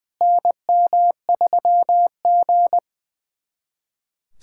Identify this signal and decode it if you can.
Morse code, heard as a keyed tone, about 20 wpm, 700 Hz. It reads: NM3G
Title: SOTA QSO Practice 20wpm